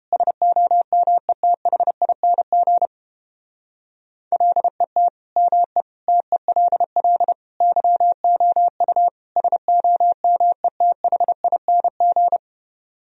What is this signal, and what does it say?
Morse code, heard as a keyed tone, about 29 wpm, 700 Hz. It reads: SOMETHING LET ME TELL YOU SOMETHING